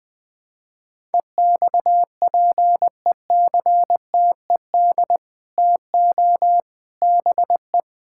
Morse code, heard as a keyed tone, about 20 wpm, 700 Hz. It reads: EXPECTED TO BE
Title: Sets of 8 Words - 20wpm